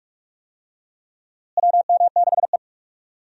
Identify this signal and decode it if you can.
Morse code, heard as a keyed tone, about 45 wpm, 700 Hz. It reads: WM6E
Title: USA Call Signs - 4 Characters - Repeated 2x - 45wpm